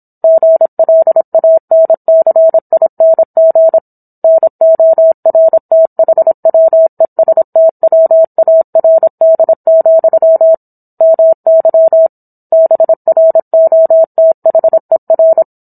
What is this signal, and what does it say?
Morse code, heard as a keyed tone, about 26 wpm, 650 Hz. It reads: GLANCING NORTHWESTWARD, MY BROTHER